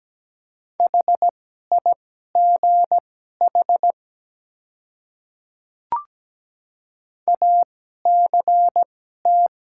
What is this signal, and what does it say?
Morse code, heard as a keyed tone, about 17 wpm, spaced as for 12 wpm, 700 Hz. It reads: HIGH ACT
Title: 2x Character Spacing - Top 200 Words - 17wpm